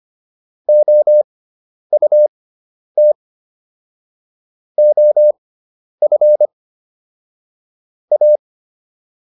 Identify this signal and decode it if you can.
Morse code, heard as a keyed tone, about 25 wpm, spaced as for 10 wpm, 600 Hz. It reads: OUT OF A